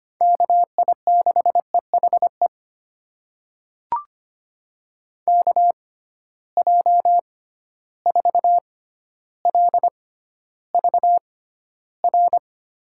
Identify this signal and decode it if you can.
Morse code, heard as a keyed tone, about 25 wpm, spaced as for 13 wpm, 700 Hz. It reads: KI6EHE KJ4LVR
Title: USA Call Signs - 6 Characters - 6x Character Spacing - 25wpm